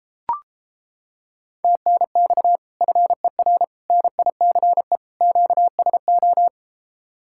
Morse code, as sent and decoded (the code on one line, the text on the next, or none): - -. -..- / ..-. .-. / -. .. -.-. / --.- ... ---
TNX FER NICE QSO